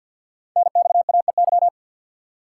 Morse code, heard as a keyed tone, about 49 wpm, 700 Hz. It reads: NXAEY
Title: Code Groups - 5 Letters - 50wpm